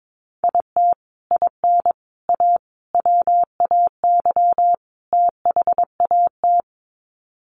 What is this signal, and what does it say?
Morse code, heard as a keyed tone, about 22 wpm, 700 Hz. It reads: IT IN A WAY THAT
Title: Sets of 5 Words Encore2 22wpm